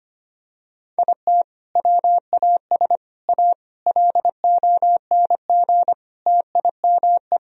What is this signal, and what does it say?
Morse code, heard as a keyed tone, about 25 wpm, 700 Hz. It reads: IT WAS A LONG TIME
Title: Sentences from Top 1000 Words - Review Part 2 - 25wpm